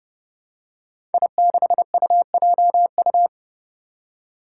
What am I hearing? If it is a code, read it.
Morse code, heard as a keyed tone, about 30 wpm, 700 Hz. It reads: I6UJU
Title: Five Challenging Letter-Number Combinations 30wpm